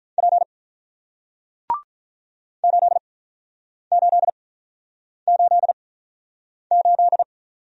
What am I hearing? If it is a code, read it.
Morse code, heard as a keyed tone, about 48 wpm, spaced as for 9 wpm, 700 Hz. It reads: P 8888